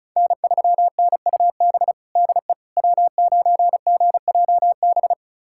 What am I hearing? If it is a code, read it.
Morse code, heard as a keyed tone, about 34 wpm, 700 Hz. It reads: N3NUB DE W9GJB
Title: USA Callsign DE Callsign - 5 Characters - 35wpm